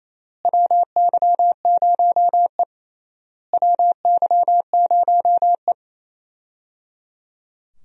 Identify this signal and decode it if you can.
Morse code, heard as a keyed tone, about 28 wpm, 700 Hz. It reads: WY0E WY0E
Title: USA Call Signs - 4 Characters - Repeated 3x - 28wpm